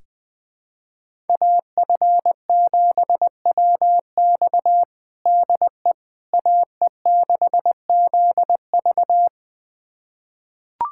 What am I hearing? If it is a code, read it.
Morse code, heard as a keyed tone, about 21 wpm, 700 Hz. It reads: AF7WX DE AE6ZV
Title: USA Callsign DE Callsign - 5 Characters - 20wpm